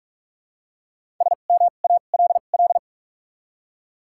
Morse code, heard as a keyed tone, about 46 wpm, spaced as for 32 wpm, 700 Hz. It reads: SMALL
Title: 2x Character Spacing - Top 200 Words - 45wpm